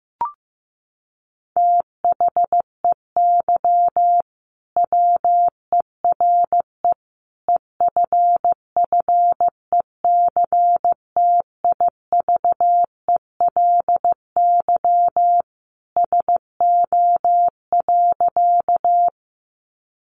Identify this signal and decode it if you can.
Morse code, heard as a keyed tone, about 15 wpm, 700 Hz. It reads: THEY WERE EFFECTIVELY SO.
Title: Sentences from Top 1400 Words 15wpm